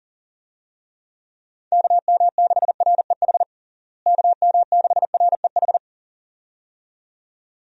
Morse code, heard as a keyed tone, about 40 wpm, 700 Hz. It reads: KM6REH KM6REH
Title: USA Call Signs - 6 Characters - Repeated 2x - 40wpm